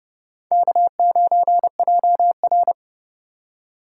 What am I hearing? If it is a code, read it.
Morse code, heard as a keyed tone, about 30 wpm, 700 Hz. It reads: K9JR
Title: USA Call Signs - 2x Character Spacing - 30wpm